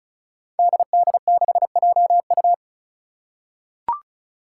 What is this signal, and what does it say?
Morse code, heard as a keyed tone, about 36 wpm, 700 Hz. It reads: DD6JU
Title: Five Challenging Letter-Number Combinations 35wpm